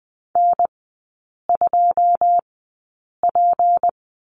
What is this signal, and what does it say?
Morse code, heard as a keyed tone, about 20 wpm, spaced as for 8 wpm, 700 Hz. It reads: N2P
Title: 152 Lesson 25e - Letter p - Three Characters 20wpm